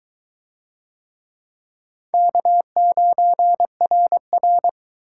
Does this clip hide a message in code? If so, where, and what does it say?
Yes: Morse code from 2.1 to 4.7 s, heard as a keyed tone, about 20 wpm, 700 Hz. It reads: K9RR